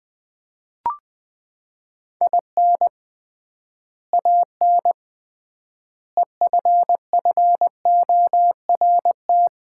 Morse code, heard as a keyed tone, about 21 wpm, 700 Hz. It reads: IN AN EFFORT
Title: 3x Word-Spacing - Sets of 4 Words - 20wpm